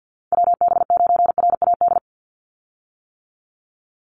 Morse code, heard as a keyed tone, about 50 wpm, 700 Hz. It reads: WB9RAD
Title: USA Call Signs 50wpm